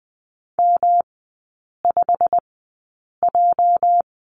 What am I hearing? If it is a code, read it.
Morse code, heard as a keyed tone, about 20 wpm, spaced as for 8 wpm, 690 Hz. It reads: M5J
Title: 205 Lesson 33e - Letter j - Three Characters 20wpm